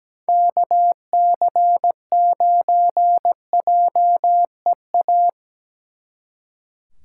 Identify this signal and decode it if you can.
Morse code, heard as a keyed tone, about 20 wpm, 700 Hz. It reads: KC9JEA
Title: Speed Racing - USA 2x3 Call Signs 17wpm